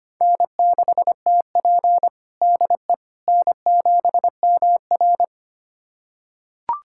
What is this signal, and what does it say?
Morse code, heard as a keyed tone, about 25 wpm, 700 Hz. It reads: N6TP DE N7MR
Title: USA Callsign DE Callsign - 4 Characters - 25wpm